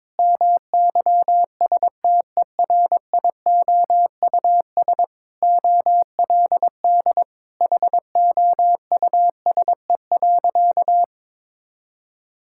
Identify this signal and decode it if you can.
Morse code, heard as a keyed tone, about 22 wpm, 700 Hz. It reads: MYSTERIOUS OLD HOUSE.
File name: practice / 22wpm / Sentences with 2-Word Phrases - 22wpm